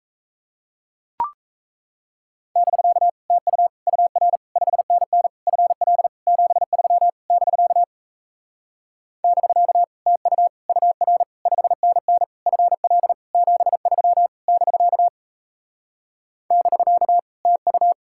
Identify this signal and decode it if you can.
Morse code, heard as a keyed tone, about 37 wpm, 700 Hz. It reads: <BK> TU UR 5NN FL 73 <BK> <BK> TU UR 5NN FL 73 <BK> <BK> TU